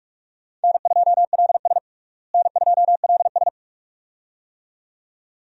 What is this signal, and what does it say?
Morse code, heard as a keyed tone, about 46 wpm, 700 Hz. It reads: N2LS N2LS
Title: USA Call Signs - 4 Characters - Repeated 2x - 45wpm